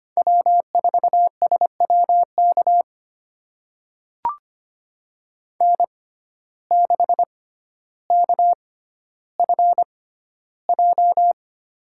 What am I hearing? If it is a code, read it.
Morse code, heard as a keyed tone, about 25 wpm, spaced as for 13 wpm, 700 Hz. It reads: W4SWK N6KFJ
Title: USA Call Signs - 5 Characters - 6x Character Spacing - 25wpm